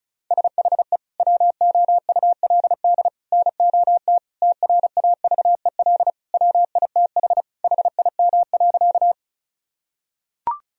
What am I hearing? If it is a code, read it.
Morse code, heard as a keyed tone, about 34 wpm, 700 Hz. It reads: SHE WOULD NOT TRAVEL WITH HIM.
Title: Sentences from Top 1000 Words - Review Part 2 - 35wpm